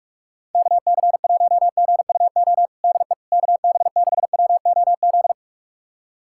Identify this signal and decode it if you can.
Morse code, heard as a keyed tone, about 46 wpm, 700 Hz. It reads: KC1GUO DE KB6WOZ
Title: USA Callsign DE Callsign 45wpm